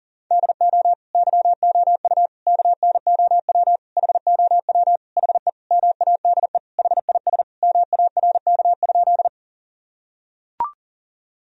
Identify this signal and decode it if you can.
Morse code, heard as a keyed tone, about 40 wpm, 700 Hz. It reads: DO YOU KNOW HOW HE MADE HIS MARK?
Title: Sentences from Top 1000 Words - Review Part 1 - 40wpm